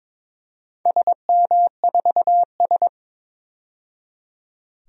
Morse code, heard as a keyed tone, about 22 wpm, 700 Hz. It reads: SM4S
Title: DXCC-100 Call Signs - 4 Characters - 22wpm